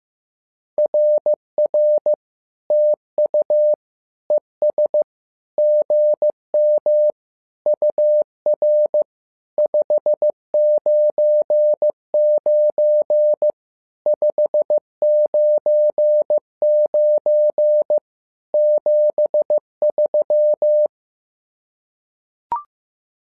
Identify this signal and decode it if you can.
Morse code, heard as a keyed tone, about 15 wpm, 600 Hz. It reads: RR TU ES GM UR 599 599 73
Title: SOTA QSO Practice 15wpm